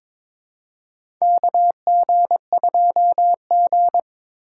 Morse code, heard as a keyed tone, about 22 wpm, 700 Hz. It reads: KG2G
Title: USA Call Signs - 4x Character Spacing - 22wpm